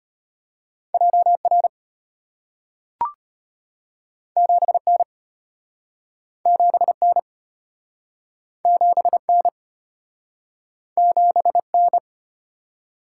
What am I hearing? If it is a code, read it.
Morse code, heard as a keyed tone, about 33 wpm, 700 Hz. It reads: JR 7N 7N 7N 7N